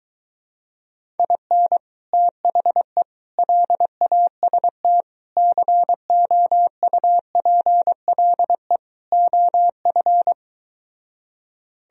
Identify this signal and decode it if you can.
Morse code, heard as a keyed tone, about 23 wpm, 700 Hz. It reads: IN THE LAST COUPLE OF